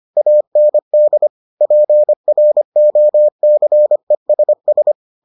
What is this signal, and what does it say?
Morse code, heard as a keyed tone, about 25 wpm, 600 Hz. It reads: AND PROCESS